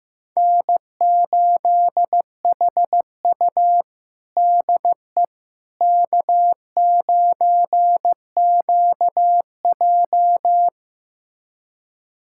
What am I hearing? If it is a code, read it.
Morse code, heard as a keyed tone, about 15 wpm, 700 Hz. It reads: N8HU DE K9QJ
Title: USA Callsign DE Callsign - 4 Characters - 15wpm